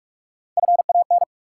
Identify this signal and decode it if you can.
Morse code, heard as a keyed tone, about 46 wpm, 700 Hz. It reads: FAN
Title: Sentences from Top 1300 Words - New Word Drill Alt - 45wpm